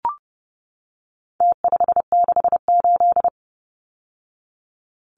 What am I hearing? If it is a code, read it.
Morse code, heard as a keyed tone, about 30 wpm, 700 Hz. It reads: T568
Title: Numbers - Contest Style - 4-Digit 30wpm